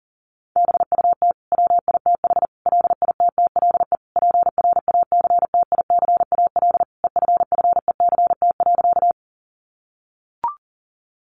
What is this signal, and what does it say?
Morse code, heard as a keyed tone, about 40 wpm, 700 Hz. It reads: BUT WITH LITTLE PRACTICAL EFFECT.
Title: Sentences from Top 1400 Words 40wpm